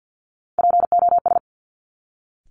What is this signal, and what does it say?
Morse code, heard as a keyed tone, about 50 wpm, 700 Hz. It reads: POS
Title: Top Three Letter Combinations in Words 50wpm